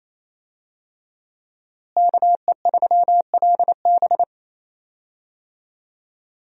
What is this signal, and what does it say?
Morse code, heard as a keyed tone, about 28 wpm, 700 Hz. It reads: KE3LB
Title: USA Call Signs - 5 Characters - Repeated 3x - 28wpm